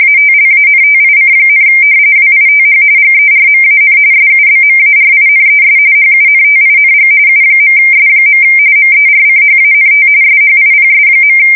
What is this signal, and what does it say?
RTTY, about 45.45 baud. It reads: ATLANTIC, WITH GERMAN SUBMARINES HUNTING U.S. MERCHANT SHIPS.
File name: fsk170 / usnavy4